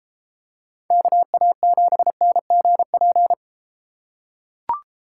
Morse code, heard as a keyed tone, about 33 wpm, 700 Hz. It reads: KA7NGP